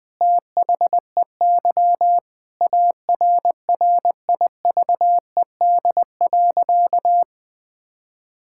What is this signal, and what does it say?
Morse code, heard as a keyed tone, about 20 wpm, 700 Hz. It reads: THEY ARRIVED.
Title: Sentences with 2-Word Phrases - 20wpm